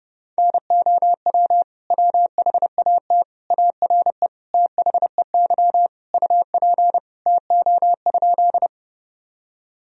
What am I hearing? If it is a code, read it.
Morse code, heard as a keyed tone, about 30 wpm, 700 Hz. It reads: NOW WHAT ARE THEY UP TO?
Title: Sentences from Top 1000 Words - Review Part 1 - 30wpm